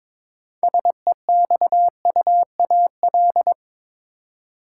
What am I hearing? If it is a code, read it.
Morse code, heard as a keyed tone, about 22 wpm, 700 Hz. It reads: SEXUAL
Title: Six Letter Words - 22wpm